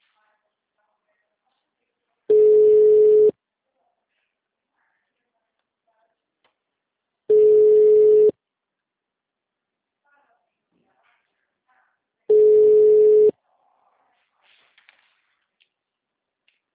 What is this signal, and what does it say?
Ringback tone (Europe, 425 Hz), with speech.